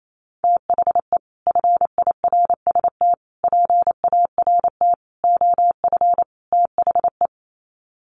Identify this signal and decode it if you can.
Morse code, heard as a keyed tone, about 28 wpm, 700 Hz. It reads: THE FIRST PART OF THE
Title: Sets of 5 Words Encore 28wpm